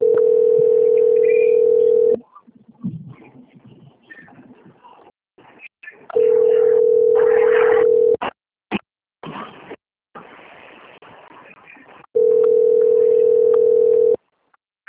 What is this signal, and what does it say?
Ringback tone (North America), 440 + 480 Hz.
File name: out- - - - - .22963